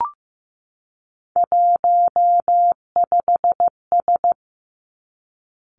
Morse code, heard as a keyed tone, about 15 wpm, 700 Hz. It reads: 15S
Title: Three Challenging Letter-Number Combinations 15wpm